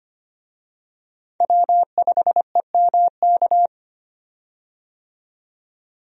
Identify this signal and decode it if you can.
Morse code, heard as a keyed tone, about 25 wpm, 700 Hz. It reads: W5EMK
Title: USA Call Signs - 5 Characters - 25wpm